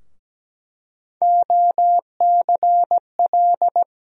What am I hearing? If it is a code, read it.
Morse code, heard as a keyed tone, about 17 wpm, 700 Hz. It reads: OCL
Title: USA Call Sign Postfixes - 3 Letters 17wpm